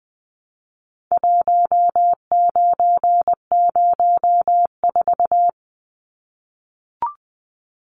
Morse code, heard as a keyed tone, about 20 wpm, 700 Hz. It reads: 1904
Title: Numbers - 4-Digit 20wpm